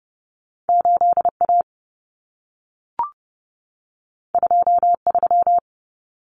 Morse code, heard as a keyed tone, about 30 wpm, 700 Hz. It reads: 8A 23
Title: Common Cut Numbers - 2-Digit 30wpm